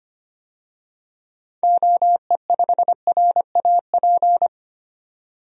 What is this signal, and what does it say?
Morse code, heard as a keyed tone, about 25 wpm, 700 Hz. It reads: OE5RAP
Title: DXCC-100 Call Signs - 6 Characters - 25wpm